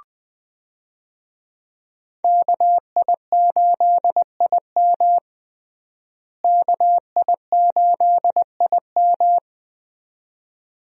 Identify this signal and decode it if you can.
Morse code, heard as a keyed tone, about 20 wpm, 700 Hz. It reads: KI8IM KI8IM
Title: USA Call Signs - 5 Characters - Repeated 2x - 20wpm